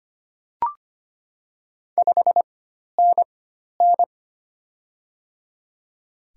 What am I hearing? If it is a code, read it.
Morse code, heard as a keyed tone, about 25 wpm, spaced as for 12 wpm, 700 Hz. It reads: 5NN
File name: Top 20 QSO Elements - 4x Character Spacing - 25wpm